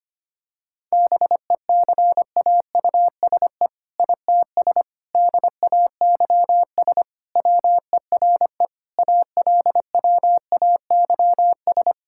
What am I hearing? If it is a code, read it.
Morse code, heard as a keyed tone, about 25 wpm, 700 Hz. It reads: BECAUSE ITS DAYS WERE ALWAYS
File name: practice / 25wpm / All-Age Appropriate Jokes - 25wpm